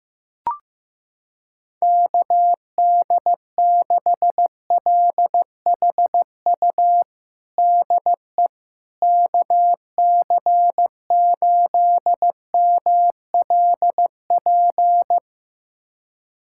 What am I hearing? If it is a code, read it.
Morse code, heard as a keyed tone, about 15 wpm, 700 Hz. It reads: KD6LHU DE KC8MLP
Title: USA Callsign DE Callsign 15wpm